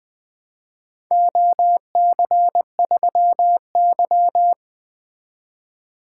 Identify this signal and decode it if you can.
Morse code, heard as a keyed tone, about 22 wpm, 700 Hz. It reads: OC3Y
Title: Speed-Racing - Four Letter-Number Combinations - 20wpm